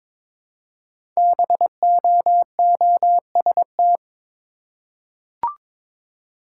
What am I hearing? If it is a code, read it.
Morse code, heard as a keyed tone, about 22 wpm, 700 Hz. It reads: BOOST
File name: 5 Letter Words - First 3 Letters Given - 22wpm